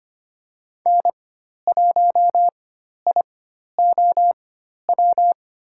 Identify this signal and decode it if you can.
Morse code, heard as a keyed tone, about 25 wpm, spaced as for 12 wpm, 700 Hz. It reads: N1IOW
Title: USA Call Signs - 4x Character Spacing - 25wpm